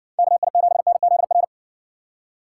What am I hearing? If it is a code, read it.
Morse code, heard as a keyed tone, about 60 wpm, 700 Hz. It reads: KI7NZR